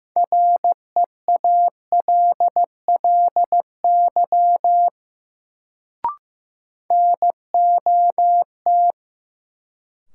Morse code, heard as a keyed tone, about 15 wpm, 700 Hz. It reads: REALLY NOT
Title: Sentences from Top 600 Words - New Word Drill - 15wpm